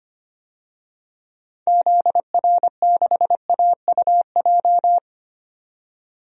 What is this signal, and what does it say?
Morse code, heard as a keyed tone, about 25 wpm, 700 Hz. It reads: ZR6AUJ
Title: DXCC-100 Call Signs - 6 Characters - 25wpm